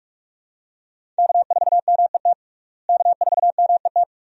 Morse code, heard as a keyed tone, about 47 wpm, 700 Hz. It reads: K4MET K4MET
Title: USA Call Signs - 5 Characters - Repeated 2x - 45wpm